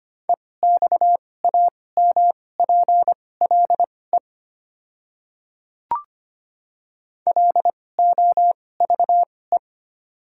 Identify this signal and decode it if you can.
Morse code, heard as a keyed tone, about 25 wpm, spaced as for 18 wpm, 700 Hz. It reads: EXAMPLE LOVE
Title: 2x Character Spacing - Top 500 Words - 25wpm